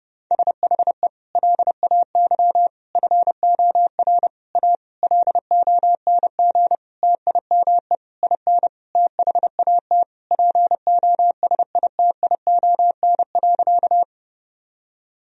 Morse code, heard as a keyed tone, about 31 wpm, 700 Hz. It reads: SHE LAY FOR A LONG TIME IN THAT POSITION.
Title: Sentences from Top 1000 Words - Review Part 2 - 30wpm